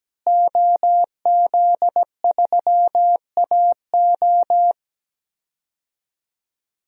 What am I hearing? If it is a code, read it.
Morse code, heard as a keyed tone, about 17 wpm, 700 Hz. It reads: OZ3AO
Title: DXCC-100 Call Signs - 5 Characters - Repeated 2x - 17wpm